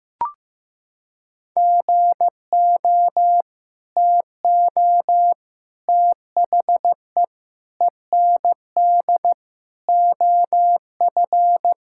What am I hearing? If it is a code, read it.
Morse code, heard as a keyed tone, about 15 wpm, 700 Hz. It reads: GO TO THE END OF
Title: Sentences from Top 1000 Words - Review Part 5 - 15wpm